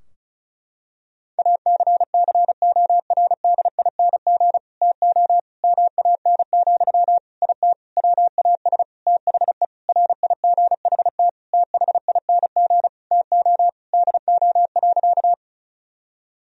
Morse code, heard as a keyed tone, about 34 wpm, 700 Hz. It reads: ACCORDING TO MAN, IT WAS THE RIGHT THING TO DO.
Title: Sentences from Top 1000 Words - Review Part 4 - 35wpm